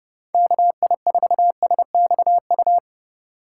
Morse code, heard as a keyed tone, about 32 wpm, 700 Hz. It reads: KI4SXU
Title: USA Call Signs - 6 Characters - 3x Character Spacing - 30wpm